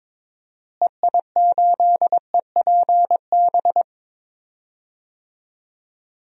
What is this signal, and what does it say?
Morse code, heard as a keyed tone, about 22 wpm, 700 Hz. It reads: EI8EPB
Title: DXCC-100 Call Signs - 6 Characters - Repeated 2x - 22wpm